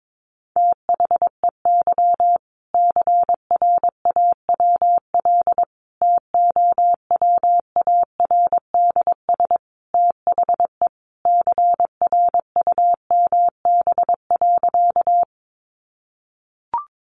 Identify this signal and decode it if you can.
Morse code, heard as a keyed tone, about 22 wpm, 700 Hz. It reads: THEY CRAWL TOWARDS THE CRUMB.
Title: Additional Easy Phrases 22wpm